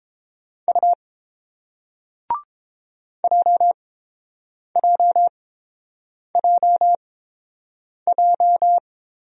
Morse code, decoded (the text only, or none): U JJJJ